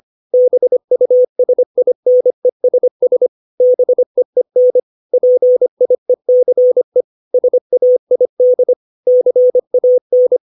Morse code, heard as a keyed tone, about 25 wpm, 500 Hz. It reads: BUSINESS BEEN PIECE SAID CAN
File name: 632 words / 632 Words 25WPM